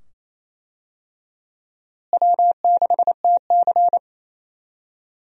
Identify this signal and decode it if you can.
Morse code, heard as a keyed tone, about 27 wpm, 700 Hz. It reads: W6TC
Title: USA Call Signs - 4 Characters - Repeated 2x - 28wpm